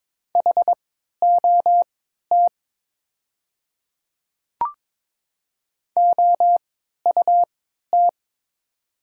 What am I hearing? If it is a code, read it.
Morse code, heard as a keyed tone, about 22 wpm, spaced as for 12 wpm, 700 Hz. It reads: HOT OUT
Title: 3x Character Spacing - Top 100 Words - 22wpm